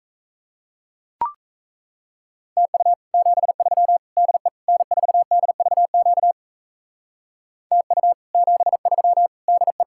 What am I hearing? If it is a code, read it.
Morse code, heard as a keyed tone, about 34 wpm, 700 Hz. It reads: TU 73 DE N4DVQ TU 73 DE
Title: Speed-Racing Common QSO Phrases Encore - 28wpm